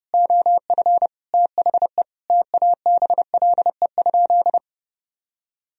Morse code, heard as a keyed tone, about 30 wpm, 700 Hz. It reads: OF THE TABLE?
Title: Sentences from Top 1000 Words - Review Part 3 - 30wpm